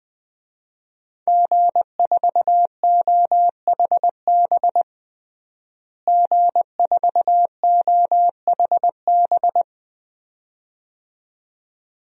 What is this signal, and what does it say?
Morse code, heard as a keyed tone, about 21 wpm, 700 Hz. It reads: G4OHB G4OHB
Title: DXCC-100 Call Signs - 5 Characters - Repeated 2x - 20wpm